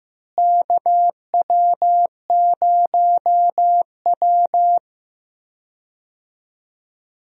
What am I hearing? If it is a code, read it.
Morse code, heard as a keyed tone, about 15 wpm, 700 Hz. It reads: KW0W
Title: USA Call Signs - 4 Characters - Repeated 2x - 15wpm